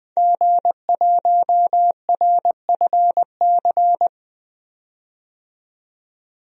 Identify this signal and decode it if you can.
Morse code, heard as a keyed tone, about 20 wpm, 700 Hz. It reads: G1RFC
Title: DXCC-100 Call Signs - 5 Characters - 20wpm